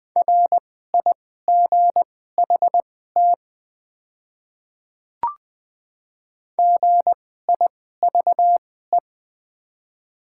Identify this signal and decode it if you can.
Morse code, heard as a keyed tone, about 20 wpm, spaced as for 14 wpm, 700 Hz. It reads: RIGHT GIVE
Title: 2x Character Spacing - Top 500 Words - 20wpm